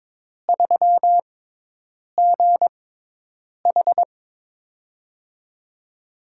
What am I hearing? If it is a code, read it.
Morse code, heard as a keyed tone, about 22 wpm, spaced as for 8 wpm, 700 Hz. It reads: 3GH
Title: USA Call Signs - 4 Characters - 6x Character Spacing - 22wpm